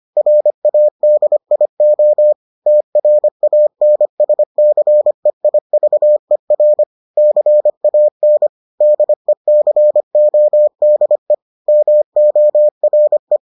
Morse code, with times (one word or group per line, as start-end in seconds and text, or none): 0.2-2.3 RADIO
2.7-6.8 TRANSCEIVER
7.2-8.5 CAN
8.8-11.3 DECODE
11.7-13.4 MORE